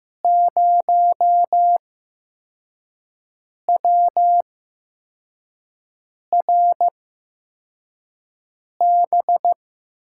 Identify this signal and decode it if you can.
Morse code, heard as a keyed tone, about 15 wpm, spaced as for 4 wpm, 700 Hz. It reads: 0WRB
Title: USA Call Signs - 5 Characters - 8x Character Spacing - 15wpm